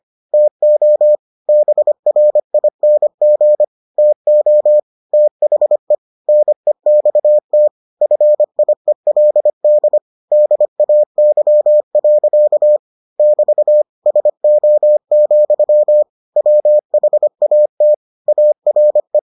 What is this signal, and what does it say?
Morse code, heard as a keyed tone, about 25 wpm, 600 Hz. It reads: TO BRING TO THE NEXT FIELD DAY. <BT> SO, WHAT ARE